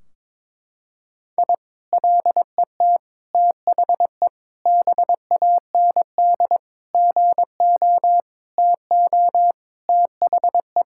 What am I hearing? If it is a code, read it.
Morse code, heard as a keyed tone, about 22 wpm, 700 Hz. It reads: I LET THE BAND GO TO THE
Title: Sentences from Top 1600 Words 22wpm